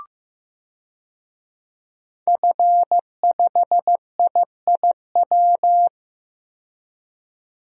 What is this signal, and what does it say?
Morse code, heard as a keyed tone, about 15 wpm, 700 Hz. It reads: F5IIW
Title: DXCC-100 Call Signs - 5 Characters - 15wpm